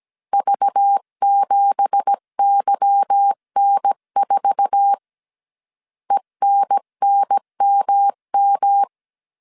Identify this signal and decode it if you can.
Morse code, heard as a keyed tone, about 17 wpm, 800 Hz. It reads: V7YN4 ENNMM
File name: CW / DET / 1 9 1500C S B 16wpm 2856